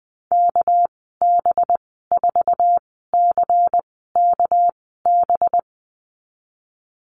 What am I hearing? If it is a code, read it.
Morse code, heard as a keyed tone, about 20 wpm, spaced as for 14 wpm, 700 Hz. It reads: KB4CKB
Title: 213 Lesson 34g - Number 8 - Call Signs 20wpm